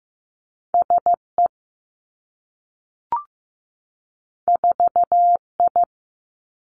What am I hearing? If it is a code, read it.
Morse code, heard as a keyed tone, about 15 wpm, 700 Hz. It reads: S E 4 I
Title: Two Challenging Letter-Number Combinations 15wpm